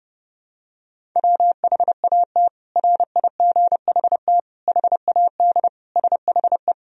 Morse code, heard as a keyed tone, about 32 wpm, 700 Hz. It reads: WHAT RIGHT HAD SHE
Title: Sentences from Top 1000 Words - Review Part 1 - 30wpm